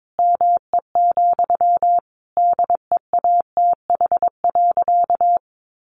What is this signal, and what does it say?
Morse code, heard as a keyed tone, about 22 wpm, 700 Hz. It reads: ME, DEATH.
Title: Sentences from Top 1100 Words 22wpm